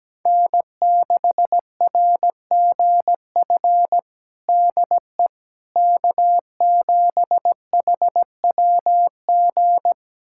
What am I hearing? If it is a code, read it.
Morse code, heard as a keyed tone, about 17 wpm, 700 Hz. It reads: N6RGF DE K7HWG
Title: USA Callsign DE Callsign - 5 Characters - 17wpm